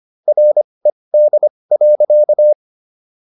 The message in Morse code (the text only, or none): RED.